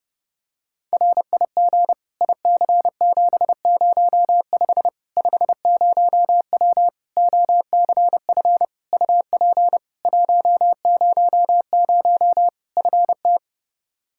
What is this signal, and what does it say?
Morse code, heard as a keyed tone, about 30 wpm, 700 Hz. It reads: RIG IC705 50W OCF UP 100 FT
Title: Practice QSO - Exchange 2 - Info - 30wpm